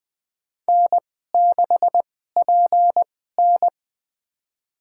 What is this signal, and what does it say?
Morse code, heard as a keyed tone, about 20 wpm, spaced as for 15 wpm, 700 Hz. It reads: N6PN
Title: USA Call Signs - 4 Characters - 2x Character Spacing - 20wpm